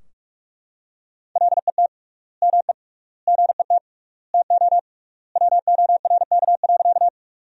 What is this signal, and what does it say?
Morse code, heard as a keyed tone, about 44 wpm, 700 Hz. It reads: LET ME GET TO WORK.
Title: 3x Word-Spacing - Sentences from Top 300 Words - 45wpm